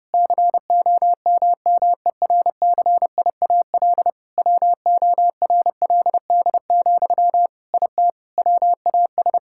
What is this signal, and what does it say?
Morse code, heard as a keyed tone, about 32 wpm, 700 Hz. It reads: COMMERCIAL WORLD, IT WAS
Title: Sentences from Top 1000 Words - Review Part 5 - 30wpm